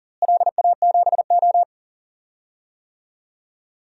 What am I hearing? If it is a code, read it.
Morse code, heard as a keyed tone, about 40 wpm, 700 Hz. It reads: LA7O
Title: DXCC-100 Call Signs - 4 Characters - Repeated 3x - 40wpm